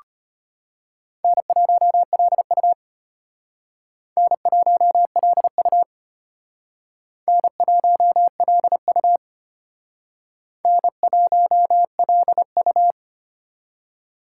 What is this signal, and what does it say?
Morse code, heard as a keyed tone, about 33 wpm, 700 Hz. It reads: N1LU N1LU N1LU N1LU